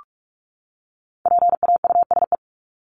Morse code, heard as a keyed tone, about 44 wpm, 700 Hz. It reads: PAUSE
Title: Top 1501-1750 Words 45wpm